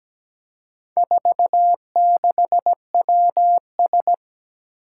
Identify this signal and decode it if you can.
Morse code, heard as a keyed tone, about 17 wpm, 700 Hz. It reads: 46WS
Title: Four Challenging Letter-Number Combinations 17wpm